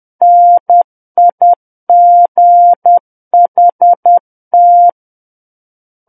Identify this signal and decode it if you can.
Morse code, heard as a keyed tone, about 10 wpm, 700 Hz. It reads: NIGHT
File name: Book One / 01-Chapter 1 - 10 WPM